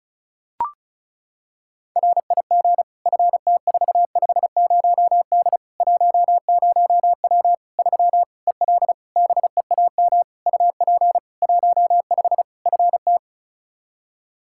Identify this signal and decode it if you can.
Morse code, heard as a keyed tone, about 36 wpm, 700 Hz. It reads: RIG FT450D 10W 3 EL BEAM UP 15 FT
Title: Practice QSO - Exchange 2 - Info - 35wpm